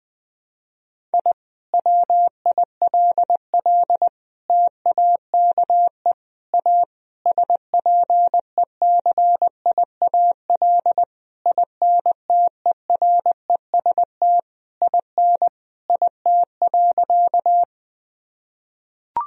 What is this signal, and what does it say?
Morse code, heard as a keyed tone, about 20 wpm, 700 Hz. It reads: I WILL TAKE A SPECIAL INTEREST IN IT.
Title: Sentences from Top 1000 Words - Review Part 3 - 20wpm